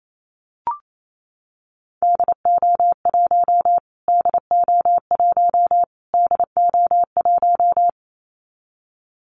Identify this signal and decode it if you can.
Morse code, heard as a keyed tone, about 28 wpm, 700 Hz. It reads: DO1 DO1 DO1
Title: Callsign Prefixes for Top 25 DXCC - Repeated 3x - 28wpm